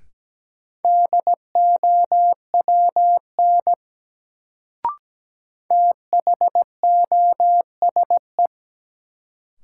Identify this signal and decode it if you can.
Morse code, heard as a keyed tone, about 17 wpm, 700 Hz. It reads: DOWN THOSE
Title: Sentences from Top 300 Words - New Word Drill - 17wpm